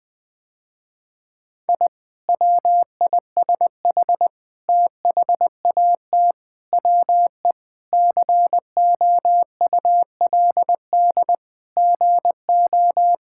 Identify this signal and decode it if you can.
Morse code, heard as a keyed tone, about 20 wpm, 700 Hz. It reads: I WISH THAT WE COULD GO
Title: Sentences with 2-Word Phrases - Spoken Beginning - 20wpm